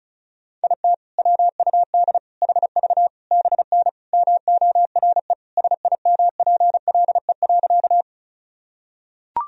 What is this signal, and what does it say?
Morse code, heard as a keyed tone, about 36 wpm, 700 Hz. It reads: IT WUD HV BN MORE SIMPLE.
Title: Sentences from Top 400 Words and Abbreviations 35wpm